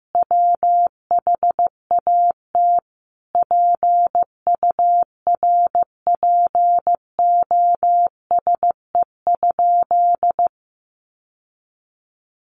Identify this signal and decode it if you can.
Morse code, heard as a keyed tone, about 15 wpm, 700 Hz. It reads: WHAT PURPOSE?
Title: Sentences from Top 800 Words 15wpm